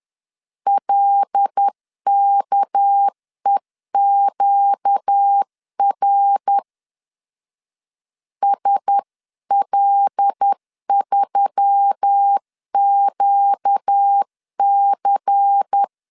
Morse code, heard as a keyed tone, about 10 wpm, 800 Hz. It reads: LKEQR SL3QC
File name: CW / DET / 1 6 1500C S C 10wpm 4252